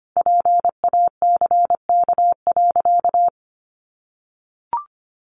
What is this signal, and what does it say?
Morse code, heard as a keyed tone, about 25 wpm, 700 Hz. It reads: PACK.
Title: Sentences from Top 1300 Words 25wpm